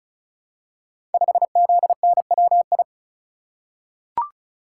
Morse code, heard as a keyed tone, about 36 wpm, 700 Hz. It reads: 5ZNWI
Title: Code Groups - 5 Letter:Numbers - 35wpm